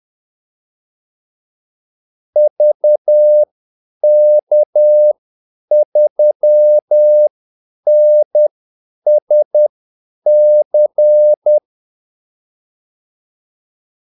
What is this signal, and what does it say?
Morse code, heard as a keyed tone, about 10 wpm, spaced as for 8 wpm, 600 Hz. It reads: VK3NSC
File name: Callsigns-10-8wpm